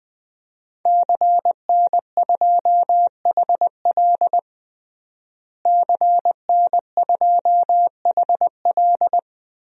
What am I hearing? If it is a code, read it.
Morse code, heard as a keyed tone, about 20 wpm, 700 Hz. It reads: CN2HL CN2HL
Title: DXCC-100 Call Signs - 5 Characters - Repeated 3x - 20wpm